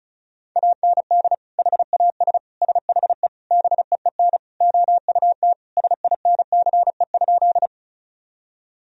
Morse code, heard as a keyed tone, about 34 wpm, 700 Hz. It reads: AND HAS SHE BEEN OUT SINCE?
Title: Sentences from Top 1000 Words - Review Part 1 - 35wpm